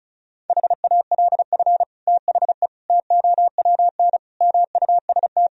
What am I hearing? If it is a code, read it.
Morse code, heard as a keyed tone, about 34 wpm, 700 Hz. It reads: HALF THE TOWN MUST
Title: Sentences from Top 1000 Words - Review Part 2 - 35wpm